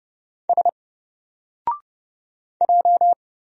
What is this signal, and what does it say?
Morse code, heard as a keyed tone, about 29 wpm, 700 Hz. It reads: S J
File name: Single-Letter-Number-30wpm